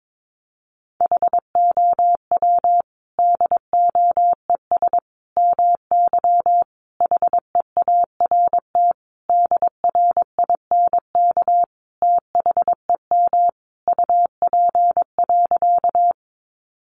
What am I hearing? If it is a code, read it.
Morse code, heard as a keyed tone, about 22 wpm, 700 Hz. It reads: HOW DOES MY HEART DRINK THEM UP.
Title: Sentences from Top 1500 Words 22wpm